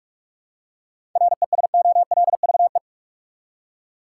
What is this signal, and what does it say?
Morse code, heard as a keyed tone, about 47 wpm, 700 Hz. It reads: RESOLVE
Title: Seven Letter Words - 45wpm